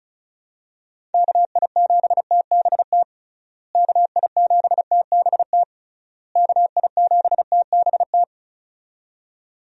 Morse code, heard as a keyed tone, about 36 wpm, 700 Hz. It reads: KI7TBT KI7TBT KI7TBT
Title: USA Call Signs - 6 Characters - Repeated 3x - 35wpm